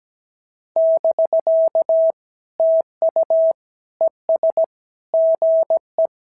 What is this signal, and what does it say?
Morse code, heard as a keyed tone, about 17 wpm, 650 Hz. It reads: <BK> TU ES GE
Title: SOTA QSO Practice 17wpm